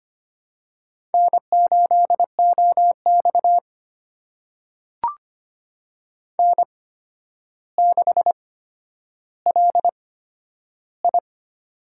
Morse code, heard as a keyed tone, about 25 wpm, spaced as for 11 wpm, 700 Hz. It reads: N8OX N6LI
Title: USA Call Signs - 4 Characters - 8x Character Spacing - 25wpm